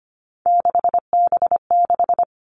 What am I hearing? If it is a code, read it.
Morse code, heard as a keyed tone, about 25 wpm, 700 Hz. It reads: 6B6
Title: All Cut Numbers - 3-Digit 25wpm